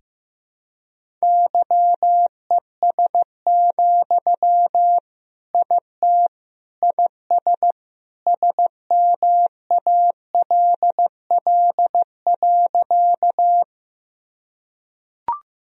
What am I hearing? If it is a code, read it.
Morse code, heard as a keyed tone, about 15 wpm, 700 Hz. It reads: YES, IT IS SMALL.
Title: Sentences from Top 1000 Words - Review Part 1 - 15wpm